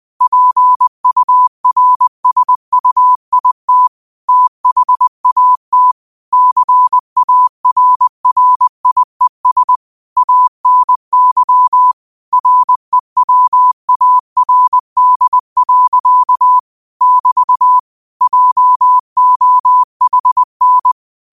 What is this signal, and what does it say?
Morse code, heard as a keyed tone, about 20 wpm, 1000 Hz. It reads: PURSUIT THAT CARRIES ANY REWARD. <BT> JOHN